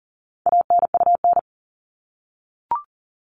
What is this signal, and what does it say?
Morse code, heard as a keyed tone, about 40 wpm, 700 Hz. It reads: ANUN
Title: All Cut Numbers - 4-Digit 40wpm